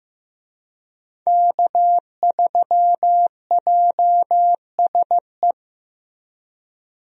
Morse code, heard as a keyed tone, about 15 wpm, 700 Hz. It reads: K3JSE
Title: USA Call Signs - 5 Characters - Repeated 3x - 15wpm